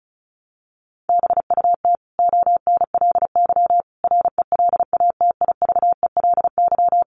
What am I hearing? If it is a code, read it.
Morse code, heard as a keyed tone, about 34 wpm, 700 Hz. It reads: BUT ONLY RELATIVELY
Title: Sentences from Top 1100 Words 35wpm